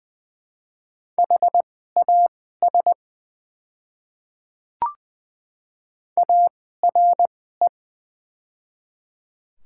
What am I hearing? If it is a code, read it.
Morse code, heard as a keyed tone, about 20 wpm, spaced as for 15 wpm, 700 Hz. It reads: HAS ARE
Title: 2x Character Spacing - Top 100 Words - 20wpm